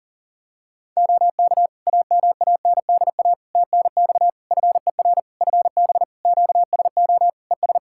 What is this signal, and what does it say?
Morse code, heard as a keyed tone, about 40 wpm, 700 Hz. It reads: OK AMANDA TNX FER FB QSO ES
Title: Practice QSO - Exchange 4 - Ending - 40wpm